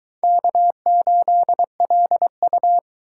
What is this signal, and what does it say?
Morse code, heard as a keyed tone, about 23 wpm, 700 Hz. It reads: K8LU